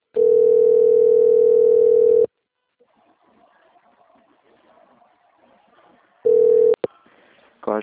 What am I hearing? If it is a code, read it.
Ringback tone (North America), 440 + 480 Hz.